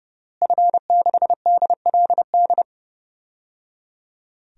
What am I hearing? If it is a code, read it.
Morse code, heard as a keyed tone, about 30 wpm, 700 Hz. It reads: F6DLD
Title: DXCC-100 Call Signs - 5 Characters - Repeated 3x - 30wpm